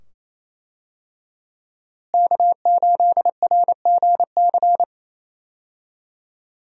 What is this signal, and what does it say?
Morse code, heard as a keyed tone, about 28 wpm, 700 Hz. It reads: K8RGC
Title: USA Call Signs - 5 Characters - Repeated 2x - 28wpm